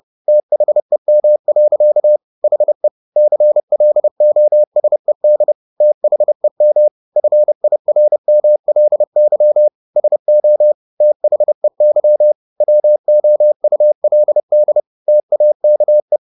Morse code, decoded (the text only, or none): THEM. HE CLOSED THEM FIRMLY SO THEY WOULD TAKE